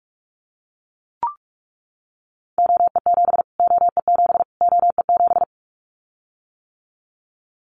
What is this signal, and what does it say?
Morse code, heard as a keyed tone, about 44 wpm, 700 Hz. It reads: OE7 OE7 OE7
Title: Callsign Prefixes for Top 25 DXCC - Repeated 3x - 45wpm